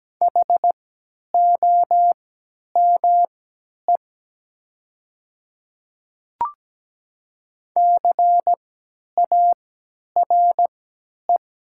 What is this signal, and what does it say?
Morse code, heard as a keyed tone, about 17 wpm, spaced as for 10 wpm, 700 Hz. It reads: HOME CARE
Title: 3x Character Spacing - Top 300 Words - 17wpm